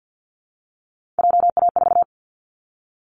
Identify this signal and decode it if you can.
Morse code, heard as a keyed tone, about 50 wpm, 700 Hz. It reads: JA4
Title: Callsign Prefixes for Top 25 DXCC - Repeated 3x - 50wpm